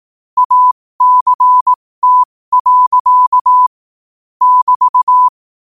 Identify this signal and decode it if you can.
Morse code, heard as a keyed tone, about 18 wpm, spaced as for 15 wpm, 1000 Hz. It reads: ACT. <BT>
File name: qotdfiles / 15wpm